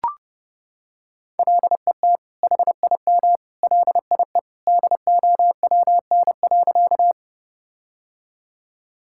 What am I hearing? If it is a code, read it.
Morse code, heard as a keyed tone, about 30 wpm, 700 Hz. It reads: LET HIM LIE DOWN.
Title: Sentences from Top 1600 Words 30wpm